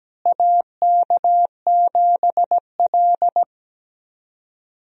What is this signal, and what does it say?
Morse code, heard as a keyed tone, about 17 wpm, 700 Hz. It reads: AK7L
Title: USA Call Signs - 4 Characters - Repeated 3x - 17wpm